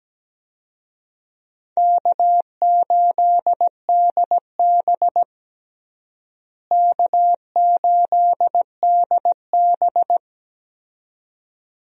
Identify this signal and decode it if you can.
Morse code, heard as a keyed tone, about 17 wpm, 700 Hz. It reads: K8DB K8DB
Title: USA Call Signs - 4 Characters - Repeated 2x - 17wpm